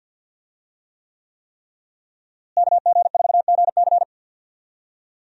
Morse code, heard as a keyed tone, about 48 wpm, 700 Hz. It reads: KG4GC
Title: USA Call Signs - 5 Characters - Repeated 3x - 50wpm